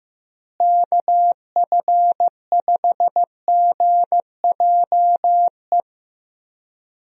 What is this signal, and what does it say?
Morse code, heard as a keyed tone, about 14 wpm, 700 Hz. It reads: KF5GJE
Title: USA Call Signs - 6 Characters - Repeated 3x - 15wpm